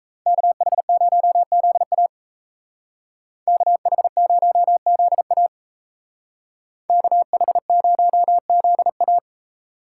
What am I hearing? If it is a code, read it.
Morse code, heard as a keyed tone, about 37 wpm, 700 Hz. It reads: KH0ZA KH0ZA KH0ZA